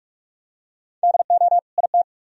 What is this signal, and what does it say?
Morse code, heard as a keyed tone, about 43 wpm, 700 Hz. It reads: DO IT